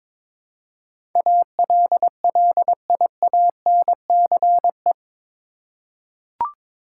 Morse code, heard as a keyed tone, about 22 wpm, 700 Hz. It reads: ALLIANCE
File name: practice / 22wpm / Eight Letter Words - 22wpm